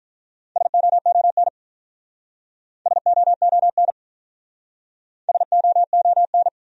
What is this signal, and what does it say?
Morse code, heard as a keyed tone, about 46 wpm, 700 Hz. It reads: SOON SOON SOON